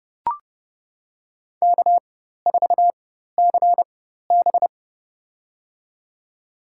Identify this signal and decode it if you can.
Morse code, heard as a keyed tone, about 30 wpm, spaced as for 14 wpm, 700 Hz. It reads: K4CB
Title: USA Call Signs - 4 Characters - 4x Character Spacing - 30wpm